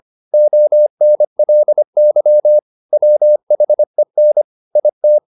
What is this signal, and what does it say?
Morse code, heard as a keyed tone, about 25 wpm, 600 Hz. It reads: ONLY WHEN IT
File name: the old man and the sea part 1 25wpm